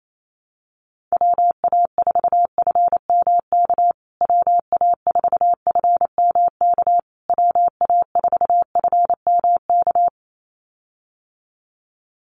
Morse code, heard as a keyed tone, about 28 wpm, 700 Hz. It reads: WA4FMK WA4FMK WA4FMK
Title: USA Call Signs Repeated 3x 28wpm